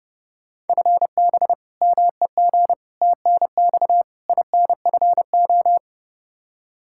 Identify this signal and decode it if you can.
Morse code, heard as a keyed tone, about 30 wpm, 700 Hz. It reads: FB MEG TNX INFO
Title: Practice QSO - Exchange 3 - More Info - 30wpm